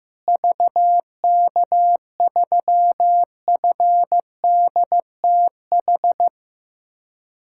Morse code, heard as a keyed tone, about 15 wpm, 700 Hz. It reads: VK3FDTH
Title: USA Callsign DE Callsign 15wpm